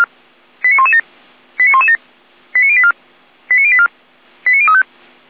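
Five-tone selective call (ZVEI), heard as quick runs of five tones, tones of about 80 ms. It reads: 89117 89117 89084 89084 89035
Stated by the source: ZVEI five-tone selective call received by SDR.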